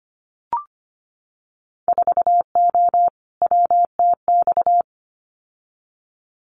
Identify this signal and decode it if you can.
Morse code, heard as a keyed tone, about 25 wpm, 700 Hz. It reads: 4O WTX
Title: Winter Field Day Exchanges Encore 25wpm